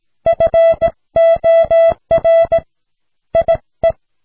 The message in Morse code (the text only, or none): FOR IE